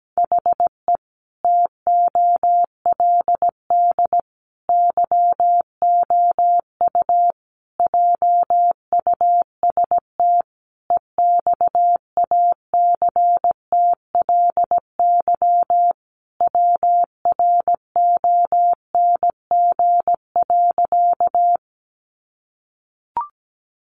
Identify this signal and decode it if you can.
Morse code, heard as a keyed tone, about 17 wpm, 700 Hz. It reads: HE TOLD YOU JUST EXACTLY WRONG.
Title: Sentences from Top 900 Words 17wpm